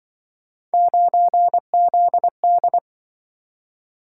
Morse code, heard as a keyed tone, about 25 wpm, 700 Hz. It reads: 9ZD
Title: Speed-Racing - Three Letter-Number Combinations - 20wpm